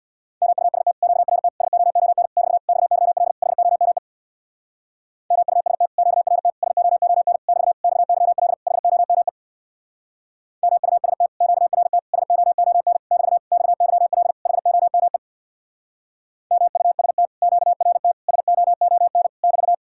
Morse code, heard as a keyed tone, about 57 wpm, 700 Hz. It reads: MUST QRT SOON <BT> XYL HOME MUST QRT SOON <BT> XYL HOME MUST QRT SOON <BT> XYL HOME MUST QRT SOON <BT>
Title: Speed-Racing Common QSO Phrases Encore - 50wpm